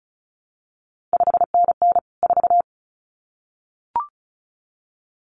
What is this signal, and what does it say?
Morse code, heard as a keyed tone, about 36 wpm, 700 Hz. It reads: 5NN 4
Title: CQ WW DX Contest Exchange - 35wpm